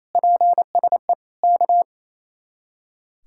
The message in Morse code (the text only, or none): PSE K